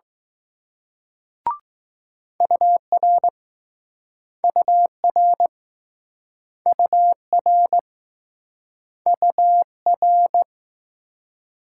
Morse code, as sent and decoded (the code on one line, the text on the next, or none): ..- .-. / ..- .-. / ..- .-. / ..- .-.
UR UR UR UR